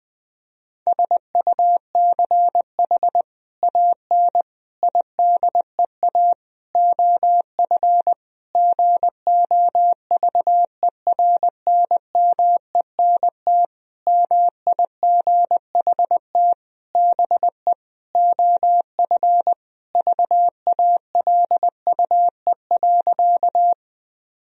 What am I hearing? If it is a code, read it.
Morse code, heard as a keyed tone, about 20 wpm, 700 Hz. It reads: SUCH AN IDEA OF GOVERNMENT MIGHT BE OF VALUE.
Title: Sentences from Top 1000 Words - Review Part 3 - 20wpm